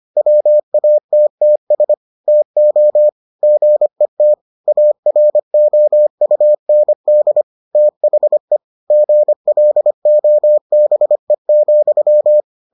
Morse code, heard as a keyed tone, about 25 wpm, 600 Hz. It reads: WATTS TO GET AROUND THE GLOBE,